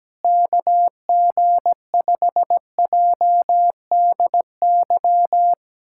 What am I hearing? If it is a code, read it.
Morse code, heard as a keyed tone, about 17 wpm, 700 Hz. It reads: KG5JDY
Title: USA Callsign DE Callsign 17wpm